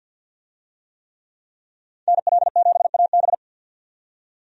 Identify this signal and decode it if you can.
Morse code, heard as a keyed tone, about 52 wpm, 700 Hz. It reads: NL7AB
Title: USA Call Signs - 5 Characters - 50wpm